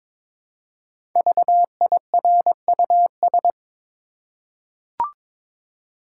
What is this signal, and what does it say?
Morse code, heard as a keyed tone, about 22 wpm, 690 Hz. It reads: VIRUS
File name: 5 Letter Words - First 3 Letters Given - 22wpm